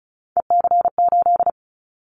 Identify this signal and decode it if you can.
Morse code, heard as a keyed tone, about 34 wpm, 700 Hz. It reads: EC8
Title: Callsign Prefixes for Top 100 DXCC - 35wpm